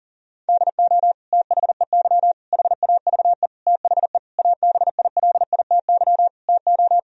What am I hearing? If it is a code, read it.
Morse code, heard as a keyed tone, about 40 wpm, 700 Hz. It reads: DO THEY HAVE THE ABILITY TO
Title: Sentences from Top 1000 Words - Review Part 3 - 40wpm